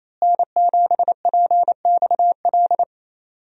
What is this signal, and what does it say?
Morse code, heard as a keyed tone, about 28 wpm, 700 Hz. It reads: N7PXL
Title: USA Call Signs - 5 Characters - Repeated 3x - 28wpm